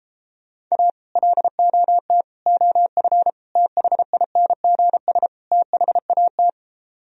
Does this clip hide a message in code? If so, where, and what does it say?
Yes: Morse code from 0.7 to 6.5 s, heard as a keyed tone, about 32 wpm, 700 Hz. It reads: A LOT OF THINGS THAT